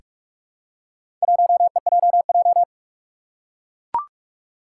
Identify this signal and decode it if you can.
Morse code, heard as a keyed tone, about 44 wpm, 700 Hz. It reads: 1EJJ
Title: Four Challenging Letter-Number Combinations 45wpm